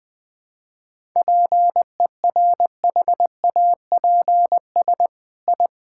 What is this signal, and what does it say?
Morse code, heard as a keyed tone, about 20 wpm, 700 Hz. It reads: PERHAPS I
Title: Sentences from Top 1000 Words - Review Part 5 - 20wpm